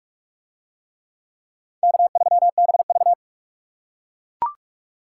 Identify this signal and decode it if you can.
Morse code, heard as a keyed tone, about 41 wpm, 700 Hz. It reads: K3BV